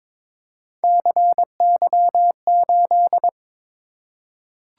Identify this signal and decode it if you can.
Morse code, heard as a keyed tone, about 23 wpm, 700 Hz. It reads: CY8
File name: practice / 22wpm / Three Letter-Number Combinations - 22wpm